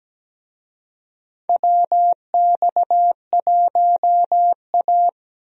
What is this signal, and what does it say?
Morse code, heard as a keyed tone, about 17 wpm, 700 Hz. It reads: WX1A
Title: USA Call Signs - 4 Characters - Repeated 2x - 17wpm